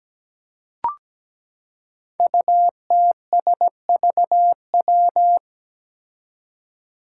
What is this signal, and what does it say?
Morse code, heard as a keyed tone, about 17 wpm, 700 Hz. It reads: UTSVW
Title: Five Challenging Letter-Number Combinations 17wpm